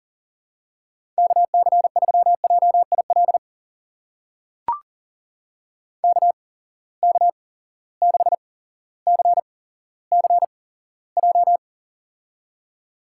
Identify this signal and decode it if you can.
Morse code, heard as a keyed tone, about 40 wpm, spaced as for 17 wpm, 700 Hz. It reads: KC3JIL KK6CCJ
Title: USA Call Signs - 6 Characters - 8x Character Spacing - 40wpm